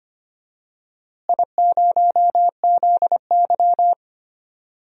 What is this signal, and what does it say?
Morse code, heard as a keyed tone, about 25 wpm, 700 Hz. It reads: I0ZY
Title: DXCC-100 Call Signs - 4 Characters - Repeated 3x - 25wpm